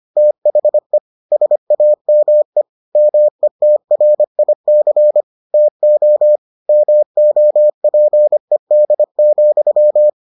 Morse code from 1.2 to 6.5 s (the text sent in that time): SAME METRIC TO